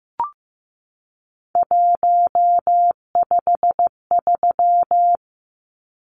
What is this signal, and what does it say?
Morse code, heard as a keyed tone, about 15 wpm, 700 Hz. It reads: 153
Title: Numbers - 3-Digit 15wpm